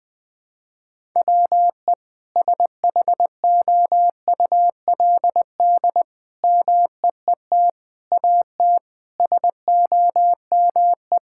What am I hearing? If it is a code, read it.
Morse code, heard as a keyed tone, about 21 wpm, 700 Hz. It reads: WE SHOULD MEET AT SOME
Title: Sentences with 7-Word Phrases - 20wpm